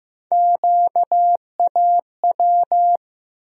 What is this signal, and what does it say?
Morse code, heard as a keyed tone, about 16 wpm, 700 Hz. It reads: QAW
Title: USA Call Sign Postfixes - 3 Letters 15wpm